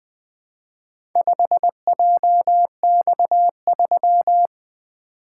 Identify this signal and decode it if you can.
Morse code, heard as a keyed tone, about 20 wpm, 700 Hz. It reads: 5JX3
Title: Four Letter-Number Combinations - 20wpm